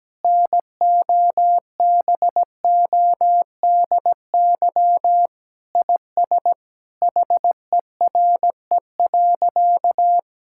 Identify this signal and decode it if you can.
Morse code, heard as a keyed tone, about 17 wpm, 700 Hz. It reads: NOBODY IS HERE.
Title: Sentences from Top 1600 Words 17wpm